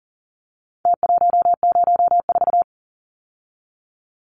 Morse code, heard as a keyed tone, about 40 wpm, 700 Hz. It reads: T104
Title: Numbers - Contest Style - 4-Digit 40wpm